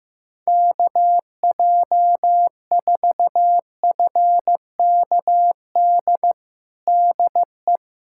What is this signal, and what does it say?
Morse code, heard as a keyed tone, about 15 wpm, 700 Hz. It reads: KJ4FKD DE
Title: USA Callsign DE Callsign 15wpm